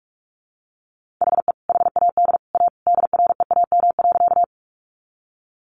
Morse code, heard as a keyed tone, about 45 wpm, 700 Hz. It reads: HE HAD A DREAM.
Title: Sentences from Top 1500 Words 45wpm